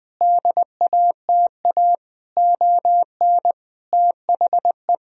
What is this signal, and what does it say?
Morse code, heard as a keyed tone, about 20 wpm, 700 Hz. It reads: DATA ON THE
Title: Sentences from Top 1000 Words - Review Part 3 - 20wpm